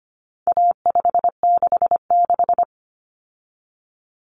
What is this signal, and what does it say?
Morse code, heard as a keyed tone, about 25 wpm, 700 Hz. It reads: A566
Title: Common Cut Numbers - 4-Digit 25wpm